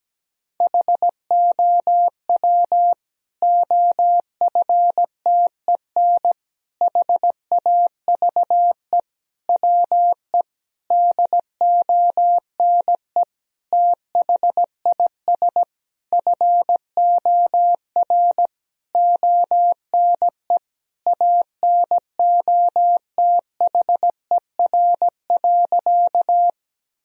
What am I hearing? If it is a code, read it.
Morse code, heard as a keyed tone, about 17 wpm, 700 Hz. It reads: HOW OFTEN HAVE WE DONE THIS FOR ONE ANOTHER.
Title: Sentences from Top 1000 Words - Review Part 1 - 17wpm